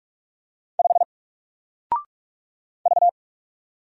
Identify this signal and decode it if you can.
Morse code, heard as a keyed tone, about 45 wpm, 700 Hz. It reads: 5 V
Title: Single Letter-Number 45wpm